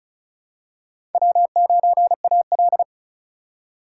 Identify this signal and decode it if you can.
Morse code, heard as a keyed tone, about 36 wpm, 700 Hz. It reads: W9AL
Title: USA Call Signs - 8x Character Spacing - 35wpm